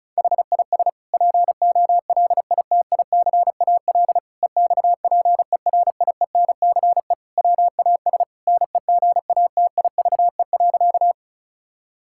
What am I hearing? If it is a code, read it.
Morse code, heard as a keyed tone, about 36 wpm, 700 Hz. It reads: HIS POLITICAL EXPERIENCE WAS NEGATIVE.
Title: Sentences from Top 1000 Words - Review Part 6 - 35wpm